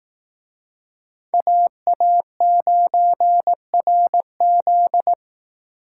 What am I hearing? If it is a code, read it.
Morse code, heard as a keyed tone, about 18 wpm, 700 Hz. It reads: AA9RZ
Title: Speed Racing - USA 2x2 Call Signs 15wpm